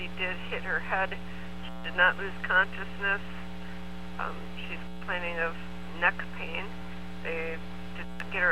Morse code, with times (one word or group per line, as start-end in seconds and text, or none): none